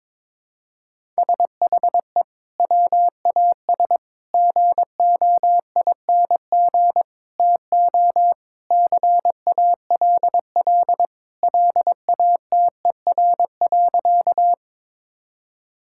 Morse code, heard as a keyed tone, about 22 wpm, 700 Hz. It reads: SHE WAS GOING TO CALL LATER.
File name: practice / 22wpm / Sentences with 4-Word Phrases - 22wpm